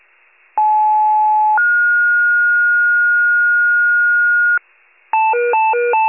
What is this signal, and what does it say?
Two-tone page, heard as a one-second tone, then a three-second tone, about 860 then 1430 Hz, with noise. It is heard from 0.6 to 4.6 s.
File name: pager / 94 RESCUE OFC / wcalert